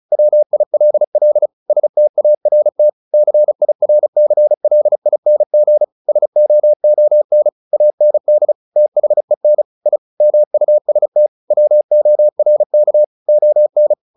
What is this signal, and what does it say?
Morse code, heard as a keyed tone, about 35 wpm, 600 Hz. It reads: WILL START CIRCLING SOON AND THEN I MUST WORK ON
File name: the old man and the sea 2 35wpm